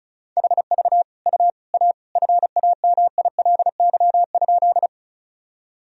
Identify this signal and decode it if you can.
Morse code, heard as a keyed tone, about 36 wpm, 700 Hz. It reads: HV U A FAMILY?
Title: Sentences from Top 400 Words and Abbreviations 35wpm